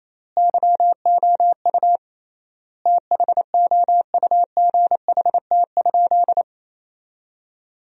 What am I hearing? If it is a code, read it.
Morse code, heard as a keyed tone, about 28 wpm, 700 Hz. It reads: YOU THOUGHT?
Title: 3x Word-Spacing - Sentences from Top 300 Words - 28wpm